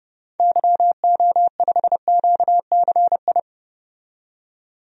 Morse code, heard as a keyed tone, about 32 wpm, 700 Hz. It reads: YO5QCI
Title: DXCC-100 Call Signs - 6 Characters - 30wpm